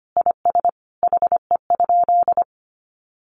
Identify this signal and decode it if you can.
Morse code, heard as a keyed tone, about 25 wpm, 700 Hz. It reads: IS HE?
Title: Sentences from Top 1300 Words 25wpm